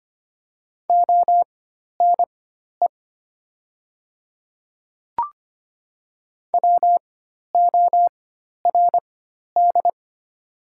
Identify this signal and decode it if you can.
Morse code, heard as a keyed tone, about 25 wpm, spaced as for 12 wpm, 700 Hz. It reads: ONE WORD
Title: 4x Character Spacing - Top 100 Words - 25wpm